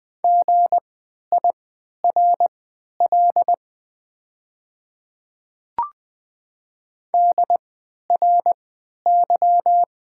Morse code, heard as a keyed tone, about 20 wpm, spaced as for 11 wpm, 700 Hz. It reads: GIRL DRY
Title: 3x Character Spacing - Top 500 Words - 20wpm